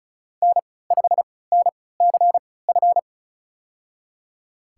Morse code, heard as a keyed tone, about 34 wpm, spaced as for 20 wpm, 700 Hz. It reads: N5NCF
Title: USA Call Signs - 5 Characters - 3x Character Spacing - 35wpm